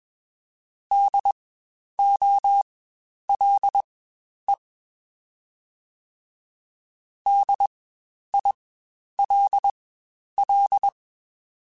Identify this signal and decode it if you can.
Morse code, heard as a keyed tone, about 21 wpm, spaced as for 10 wpm, 800 Hz. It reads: DOLE DILL
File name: Spaced Alphas / Spaced Alphas mp3s / Ds sp @20